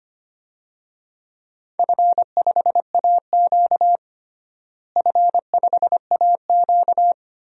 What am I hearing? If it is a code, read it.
Morse code, heard as a keyed tone, about 25 wpm, 700 Hz. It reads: F5AQ F5AQ
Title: DXCC-100 Call Signs - 4 Characters - Repeated 3x - 25wpm